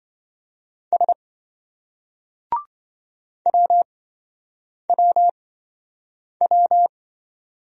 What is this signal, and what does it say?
Morse code, heard as a keyed tone, about 28 wpm, spaced as for 7 wpm, 700 Hz. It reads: S WWW